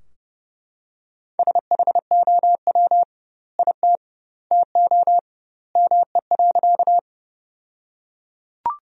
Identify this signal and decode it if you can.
Morse code, heard as a keyed tone, about 30 wpm, 700 Hz. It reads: SHOW IT TO ME.
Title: 2x Word-Spacing - Sentences from Top 200 Words - 30wpm